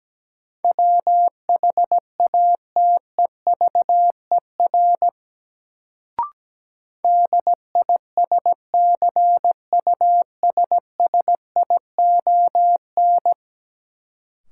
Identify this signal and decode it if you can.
Morse code, heard as a keyed tone, about 17 wpm, 700 Hz. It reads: WHATEVER DISCUSSION
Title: Sentences from Top 1100 Words - New Word Drill - 17wpm